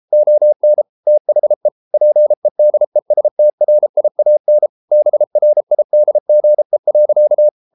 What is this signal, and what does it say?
Morse code, heard as a keyed tone, about 33 wpm, 600 Hz. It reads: ON THE PEDESTRIAN BRIDGE.